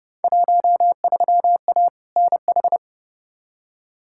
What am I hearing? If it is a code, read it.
Morse code, heard as a keyed tone, about 30 wpm, 700 Hz. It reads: 13A NH
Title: Field Day Exchanges Encore - 30wpm